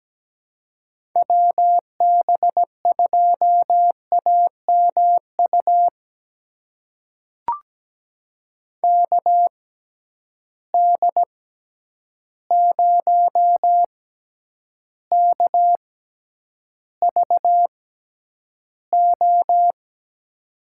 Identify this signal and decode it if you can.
Morse code, heard as a keyed tone, about 17 wpm, spaced as for 9 wpm, 700 Hz. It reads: WB2AMU KD0KVO
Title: USA Call Signs - 6 Characters - 6x Character Spacing - 17wpm